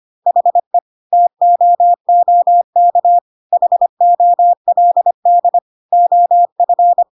Morse code, heard as a keyed tone, about 25 wpm, 700 Hz. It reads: HE TOOK HOLD OF
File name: Pinocchio / Pinocchio ch3 25 25 ik1hge